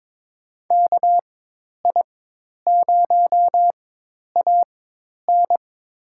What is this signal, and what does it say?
Morse code, heard as a keyed tone, about 22 wpm, spaced as for 10 wpm, 700 Hz. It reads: KI0AN
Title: USA Call Signs - 5 Characters - 4x Character Spacing - 22wpm